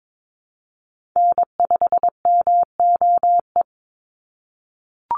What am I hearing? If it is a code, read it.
Morse code, heard as a keyed tone, about 22 wpm, 700 Hz. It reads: N5MOE
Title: USA Call Signs 22wpm